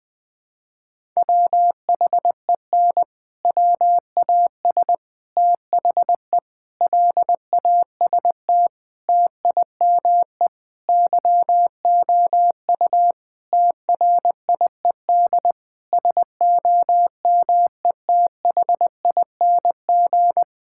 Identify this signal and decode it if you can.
Morse code, heard as a keyed tone, about 20 wpm, 700 Hz. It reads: WHEN WAS THE LAST TIME YOU TRIED SOMETHING
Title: Sentences with 6-Word Phrases - 20wpm